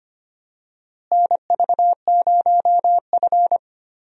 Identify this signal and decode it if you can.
Morse code, heard as a keyed tone, about 25 wpm, 700 Hz. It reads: NV0F
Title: USA Call Signs - 4 Characters - Repeated 3x - 25wpm